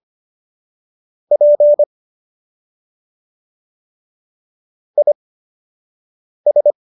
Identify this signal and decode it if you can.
Morse code, heard as a keyed tone, about 25 wpm, spaced as for 6 wpm, 590 Hz. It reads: P IS
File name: qso012 06